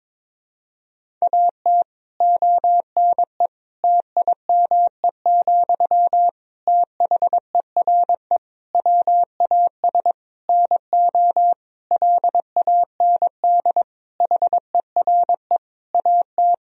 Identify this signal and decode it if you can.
Morse code, heard as a keyed tone, about 22 wpm, 700 Hz. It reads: AT ONE TIME, THERE WAS NO LAND HERE AT
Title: Sentences from Top 1000 Words - Review Part 1 - 22wpm